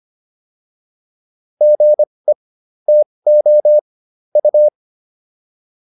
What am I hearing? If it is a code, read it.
Morse code, heard as a keyed tone, about 25 wpm, spaced as for 20 wpm, 600 Hz. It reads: GE TO U